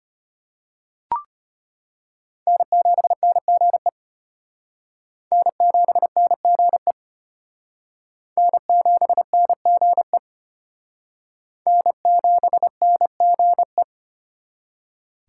Morse code, heard as a keyed tone, about 32 wpm, 700 Hz. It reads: N7NGE N7NGE N7NGE N7NGE